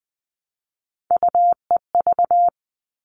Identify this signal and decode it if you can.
Morse code, heard as a keyed tone, about 20 wpm, 700 Hz. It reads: UEV
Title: All Cut Numbers - 3-Digit 20wpm